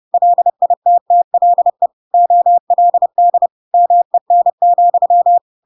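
Morse code, heard as a keyed tone, about 30 wpm, 700 Hz. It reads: LITTLE OLD MEN,
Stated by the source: Chapter 2 of The Adventures of Pinocchio read as a book in Morse code (CW) at 30 wpm, by ik1hge.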